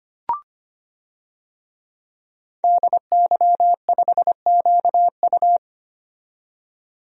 Morse code, heard as a keyed tone, about 25 wpm, 700 Hz. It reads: DY5QU
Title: DXCC-100 Call Signs - 5 Characters - 25wpm